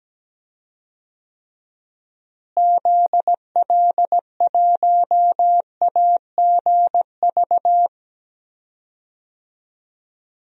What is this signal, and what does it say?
Morse code, heard as a keyed tone, about 17 wpm, 700 Hz. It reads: ZL1AGV
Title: DXCC-100 Call Signs - 6 Characters - Repeated 3x - 17wpm